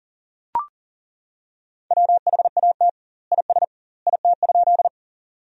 Morse code, heard as a keyed tone, about 40 wpm, 700 Hz. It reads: WHAT IS IT?
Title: 2x Word-Spacing - Sentences from Top 100 Words - 40wpm